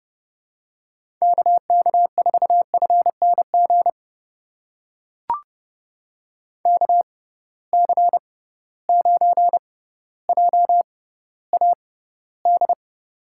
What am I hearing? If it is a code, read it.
Morse code, heard as a keyed tone, about 30 wpm, spaced as for 15 wpm, 700 Hz. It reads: KK4FNG KC9JAD
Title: USA Call Signs - 6 Characters - 6x Character Spacing - 30wpm